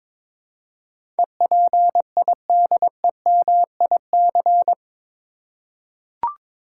Morse code, heard as a keyed tone, about 22 wpm, 700 Hz. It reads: EPIDEMIC
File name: practice / 22wpm / Eight Letter Words - 22wpm